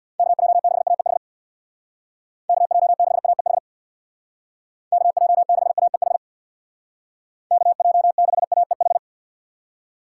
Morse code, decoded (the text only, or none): KJ6AEH KJ6AEH KJ6AEH KJ6AEH